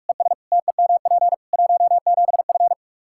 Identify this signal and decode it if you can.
Morse code, heard as a keyed tone, about 46 wpm, 700 Hz. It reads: ES TEMP 17F
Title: Practice QSO - Exchange 2 - Info - 45wpm